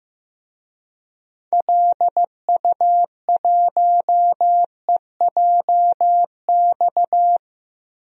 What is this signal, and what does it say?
Morse code, heard as a keyed tone, about 15 wpm, 700 Hz. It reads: LU1EJX
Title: DXCC-100 Call Signs - 6 Characters - 15wpm